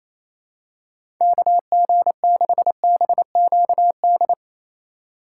Morse code, heard as a keyed tone, about 28 wpm, 700 Hz. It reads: KG6BQD
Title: USA Call Signs - 6 Characters - 3x Character Spacing - 28wpm